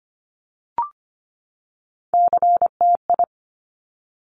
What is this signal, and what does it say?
Morse code, heard as a keyed tone, about 25 wpm, 700 Hz. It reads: CTI
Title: Top Three Letter Combinations in Words 25wpm